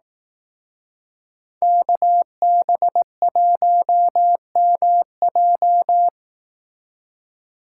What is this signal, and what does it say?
Morse code, heard as a keyed tone, about 18 wpm, 700 Hz. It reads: KB1MJ
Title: Speed Racing - USA 2x2 Call Signs 15wpm